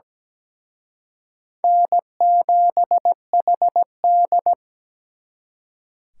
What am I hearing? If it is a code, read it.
Morse code, heard as a keyed tone, about 20 wpm, 700 Hz. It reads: N7HD
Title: Speed Racing - USA 1x2 Call Signs 17wpm